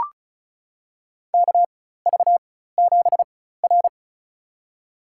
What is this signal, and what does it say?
Morse code, heard as a keyed tone, about 36 wpm, spaced as for 16 wpm, 700 Hz. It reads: KV7R
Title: USA Call Signs - 4 Characters - 4x Character Spacing - 35wpm